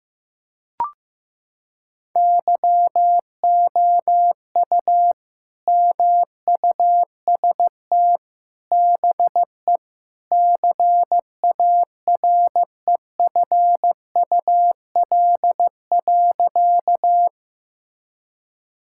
Morse code, heard as a keyed tone, about 15 wpm, 700 Hz. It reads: YOU MUST BE CAREFUL.
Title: Sentences from Top 1600 Words 15wpm